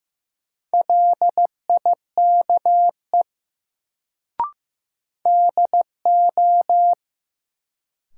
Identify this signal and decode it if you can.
Morse code, heard as a keyed tone, about 15 wpm, 700 Hz. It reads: LIKE DO
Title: Sentences from Top 100 Words - New Word Drill - 15wpm